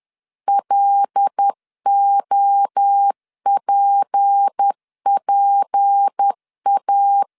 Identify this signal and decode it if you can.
Morse code, heard as a keyed tone, about 10 wpm, 800 Hz. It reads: LOPPA
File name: LEZIONI / 6 PKX0 / 6 P 10wpm 2432